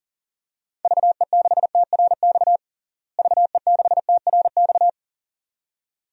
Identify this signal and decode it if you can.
Morse code, heard as a keyed tone, about 40 wpm, 700 Hz. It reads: VE6TRX VE6TRX
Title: DXCC-100 Call Signs - 6 Characters - Repeated 2x - 40wpm